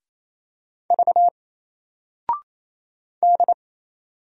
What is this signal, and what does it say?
Morse code, heard as a keyed tone, about 27 wpm, 700 Hz. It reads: V D
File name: Single-Letter-Number-28wpm